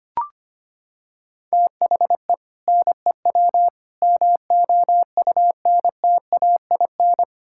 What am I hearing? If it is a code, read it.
Morse code, heard as a keyed tone, about 25 wpm, 700 Hz. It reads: THE NEW MOUNTAIN
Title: Sentences from Top 1600 Words 25wpm